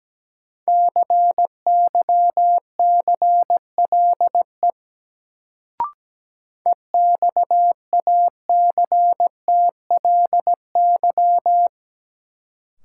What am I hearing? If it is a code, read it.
Morse code, heard as a keyed tone, about 17 wpm, 700 Hz. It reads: CYCLE EXACTLY
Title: Sentences from Top 900 Words - New Word Drill - 17wpm